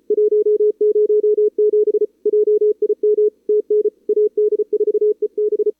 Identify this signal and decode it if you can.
Morse code, heard as a keyed tone, about 33 wpm, 440 Hz. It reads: 107 JIM TN AD4EB
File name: CWSprint / Feb2022 / k5tr-feb-na-sprint-cw-k5ot -0000